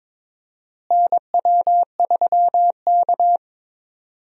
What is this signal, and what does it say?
Morse code, heard as a keyed tone, about 22 wpm, 700 Hz. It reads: NW3K
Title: USA Call Signs - 4 Characters - Repeated 2x - 22wpm